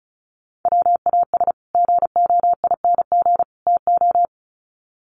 Morse code, heard as a keyed tone, about 34 wpm, 700 Hz. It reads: WAS GOING TO
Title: Sets of 3 Words 35wpm